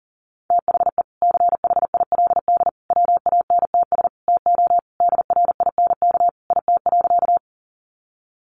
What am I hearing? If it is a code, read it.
Morse code, heard as a keyed tone, about 40 wpm, 700 Hz. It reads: THE CHILD WANTS TO DRINK IT.
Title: Additional Easy Phrases 40wpm